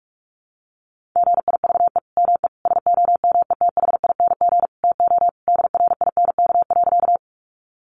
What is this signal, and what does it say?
Morse code, heard as a keyed tone, about 44 wpm, 700 Hz. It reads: GIVE ME SOMETHING TO DRINK.
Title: Sentences from Top 1500 Words 45wpm